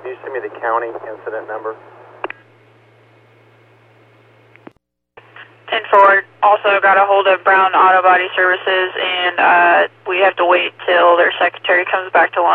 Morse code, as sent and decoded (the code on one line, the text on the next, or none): none